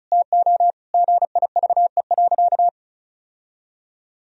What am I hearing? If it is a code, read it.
Morse code, heard as a keyed tone, about 34 wpm, 700 Hz. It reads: TO GIVE.
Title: Sentences from Top 1000 Words - Review Part 4 - 35wpm